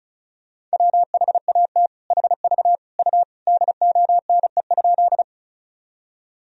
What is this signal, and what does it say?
Morse code, heard as a keyed tone, about 34 wpm, 700 Hz. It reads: WHAT HV U DONE?
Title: Sentences from Top 500 Words and Abbreviations 35wpm